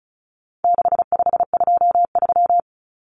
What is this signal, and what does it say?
Morse code, heard as a keyed tone, about 34 wpm, 700 Hz. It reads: 6523
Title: Numbers - 4-Digit 35wpm